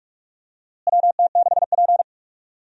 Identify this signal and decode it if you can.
Morse code, heard as a keyed tone, about 46 wpm, 700 Hz. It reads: WT6P
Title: USA Call Signs - 4 Characters - Repeated 2x - 45wpm